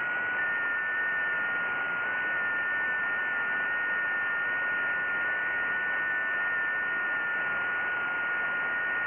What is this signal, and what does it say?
FT8.
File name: IC-7100